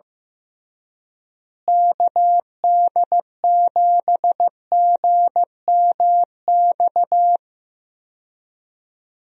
Morse code, heard as a keyed tone, about 15 wpm, 700 Hz. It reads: KD7GMX
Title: USA Call Signs - 6 Characters - Repeated 3x - 15wpm